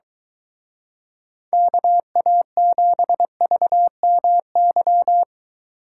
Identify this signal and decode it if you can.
Morse code, heard as a keyed tone, about 23 wpm, 700 Hz. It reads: KA7VMY